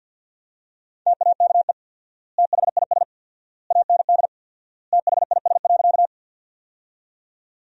Morse code, heard as a keyed tone, about 50 wpm, 700 Hz. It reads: TAKE THIS AND THIS.
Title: 4x Word-Spacing - Sentences from Top 200 Words - 50wpm